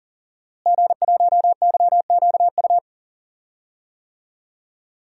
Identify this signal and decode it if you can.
Morse code, heard as a keyed tone, about 40 wpm, 700 Hz. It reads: G1YQU
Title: DXCC-100 Call Signs - 5 Characters - Repeated 2x - 40wpm